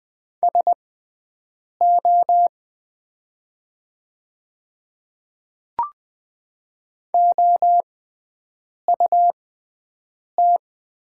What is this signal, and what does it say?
Morse code, heard as a keyed tone, about 20 wpm, spaced as for 7 wpm, 700 Hz. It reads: SO OUT
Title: 6x Character Spacing - Top 100 Words - 20wpm